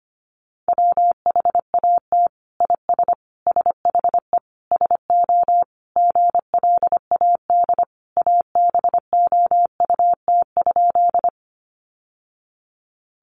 Morse code, heard as a keyed tone, about 25 wpm, 700 Hz. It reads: WHAT IS SHE SO GLAD ABOUT?
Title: Sentences from Top 800 Words 25wpm